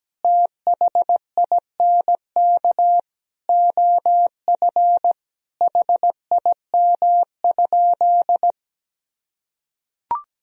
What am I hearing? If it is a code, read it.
Morse code, heard as a keyed tone, about 17 wpm, 700 Hz. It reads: THINK OF HIM?
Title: Sentences from Top 200 Words and Abbreviations 17wpm